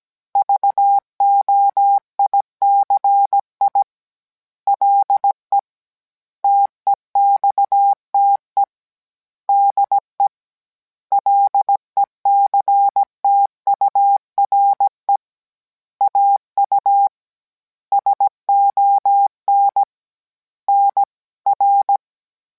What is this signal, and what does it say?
Morse code, heard as a keyed tone, about 17 wpm, 800 Hz. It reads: VOICI LE TEXTE DE LECTURE AU SON NR